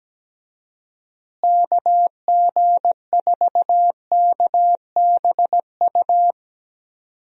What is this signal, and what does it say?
Morse code, heard as a keyed tone, about 17 wpm, 700 Hz. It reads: KG4KBU
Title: USA Call Signs - 6 Characters - Repeated 2x - 17wpm